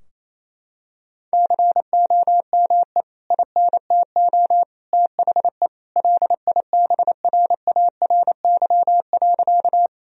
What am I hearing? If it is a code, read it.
Morse code, heard as a keyed tone, about 28 wpm, 700 Hz. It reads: COME INTO THE LIBRARY.
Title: Sentences from Top 1000 Words - Review Part 4 - 28wpm